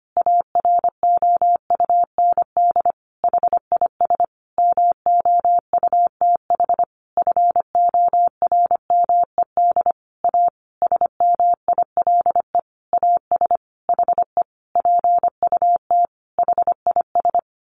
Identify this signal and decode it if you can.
Morse code, heard as a keyed tone, about 25 wpm, 700 Hz. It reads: AROUND HIS MOUTH FORMED A SMILE AS HE PUT HIS
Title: A Sound Of Thunder 25wpm